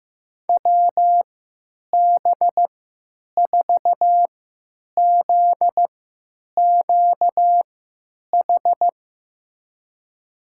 Morse code, heard as a keyed tone, about 15 wpm, spaced as for 9 wpm, 700 Hz. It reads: WB4ZQH
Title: USA Call Signs - 6 Characters - 3x Character Spacing - 15wpm